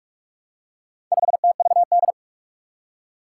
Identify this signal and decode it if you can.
Morse code, heard as a keyed tone, about 42 wpm, 700 Hz. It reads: 5TVD
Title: Four Challenging Letter-Number Combinations 45wpm